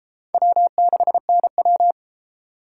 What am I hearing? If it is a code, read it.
Morse code, heard as a keyed tone, about 30 wpm, 700 Hz. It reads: W6NW